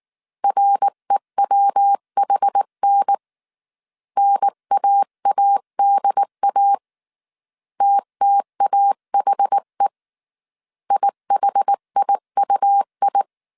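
Morse code, heard as a keyed tone, about 19 wpm, 800 Hz. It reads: REWHN NAADA TTAHE IHIUI
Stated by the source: Morse code dictation at 18 wpm from the IW3SGT QRS Morse code training course, five-character groups of 300 characters.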